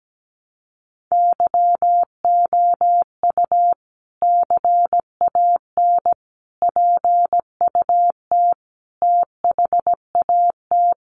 Morse code, heard as a keyed tone, about 17 wpm, 700 Hz. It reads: YOU CAN PUT THAT
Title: Sentences from Top 1500 Words 17wpm